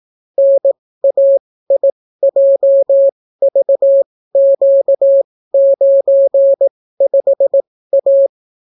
Morse code, heard as a keyed tone, about 18 wpm, spaced as for 14 wpm, 550 Hz. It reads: NAIJVQ95A